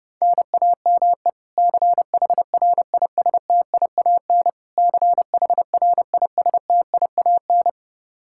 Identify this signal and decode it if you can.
Morse code, heard as a keyed tone, about 30 wpm, 700 Hz. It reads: NAME CHRISTIAN CHRISTIAN
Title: Practice QSO - Exchange 1 - Beginning - 30wpm